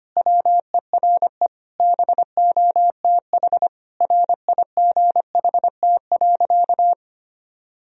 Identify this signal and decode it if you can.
Morse code, heard as a keyed tone, about 25 wpm, 700 Hz. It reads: WERE BOTH RIGHT.
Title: Sentences from Top 1000 Words - Review Part 1 - 25wpm